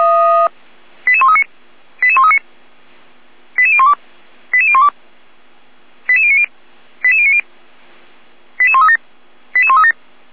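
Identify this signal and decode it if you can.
Five-tone selective call (ZVEI), heard as quick runs of five tones, tones of about 70 ms. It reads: 88139 88139 88012 88012 88090 88090 88137 88137
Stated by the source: ZVEI five-tone call 88040 received on SDR.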